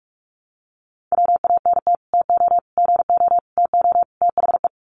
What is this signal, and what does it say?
Morse code, heard as a keyed tone, about 44 wpm, 700 Hz. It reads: WANT TO GO TO THE
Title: Sets of 5 Words Encore2 45wpm